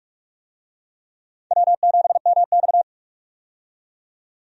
Morse code, heard as a keyed tone, about 46 wpm, 700 Hz. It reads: W7MX
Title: USA Call Signs - 4 Characters - 45wpm